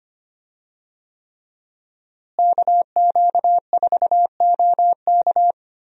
Morse code, heard as a keyed tone, about 25 wpm, 700 Hz. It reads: KQ4OK
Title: USA Call Signs - 5 Characters - Repeated 3x - 25wpm